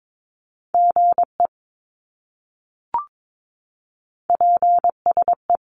Morse code, heard as a keyed tone, about 22 wpm, 700 Hz. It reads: GE PSE
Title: Abbreviations Repeated 3x - 22wpm